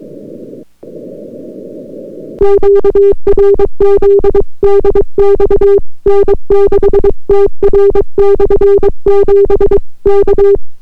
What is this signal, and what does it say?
Morse code, heard as a keyed tone, about 22 wpm, 400 Hz. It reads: QRZ DX N6TR/7 K